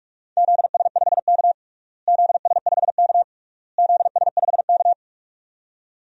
Morse code, heard as a keyed tone, about 46 wpm, 700 Hz. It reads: ZS5K ZS5K ZS5K
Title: DXCC-100 Call Signs - 4 Characters - Repeated 3x - 45wpm